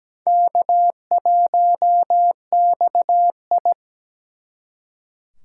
Morse code, heard as a keyed tone, about 20 wpm, 700 Hz. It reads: K1XI
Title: Speed Racing - USA 1x2 Call Signs 17wpm